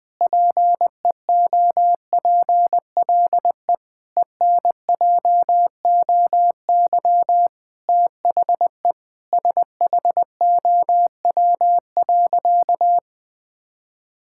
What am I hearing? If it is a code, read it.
Morse code, heard as a keyed tone, about 20 wpm, 700 Hz. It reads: PEOPLE ENJOY THE SHOW.
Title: Sentences with 4-Word Phrases - 20wpm